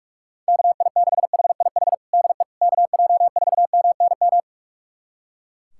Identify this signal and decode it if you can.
Morse code, heard as a keyed tone, about 41 wpm, 700 Hz. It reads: KI6HIH DE KJ4MNM
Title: USA Callsign DE Callsign 45wpm